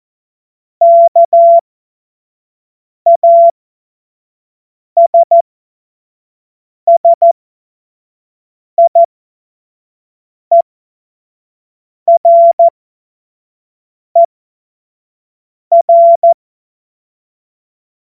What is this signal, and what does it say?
Morse code, heard as a keyed tone, about 14 wpm, spaced as for 5 wpm, 680 Hz. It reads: KASSIERER
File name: Morsekurs / Witze / Witz 10. August 2021, 25 F70